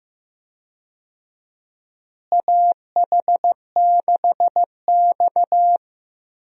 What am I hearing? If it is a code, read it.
Morse code, heard as a keyed tone, about 15 wpm, 700 Hz. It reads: AH6X
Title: USA Call Signs - 4 Characters - Repeated 2x - 15wpm